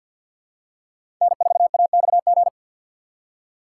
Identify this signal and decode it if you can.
Morse code, heard as a keyed tone, about 47 wpm, 700 Hz. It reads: N4AXG
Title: USA Call Signs - 5 Characters - 2x Character Spacing - 50wpm